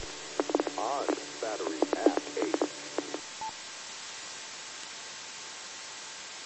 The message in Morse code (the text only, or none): none